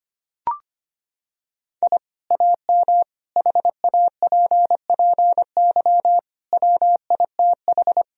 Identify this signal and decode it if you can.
Morse code, heard as a keyed tone, about 25 wpm, 700 Hz. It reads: I AM HAPPY WITH
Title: Sentences from Top 1000 Words - Review Part 2 - 25wpm